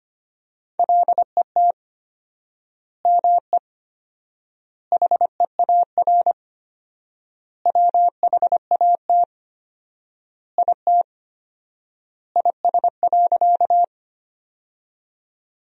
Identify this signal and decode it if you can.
Morse code, heard as a keyed tone, about 25 wpm, 700 Hz. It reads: LET ME HEAR WHAT IT IS.
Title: 4x Word-Spacing - Sentences from Top 300 Words - 25wpm